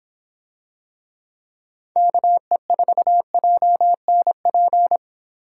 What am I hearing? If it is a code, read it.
Morse code, heard as a keyed tone, about 24 wpm, 700 Hz. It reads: KE4JNP